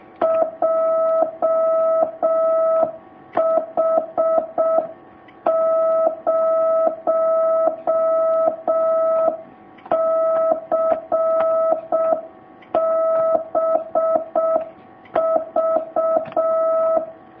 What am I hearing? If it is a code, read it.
Morse code, heard as a keyed tone, about 6 wpm, 650 Hz. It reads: JH0CBV
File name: jh8cbh / 2019cw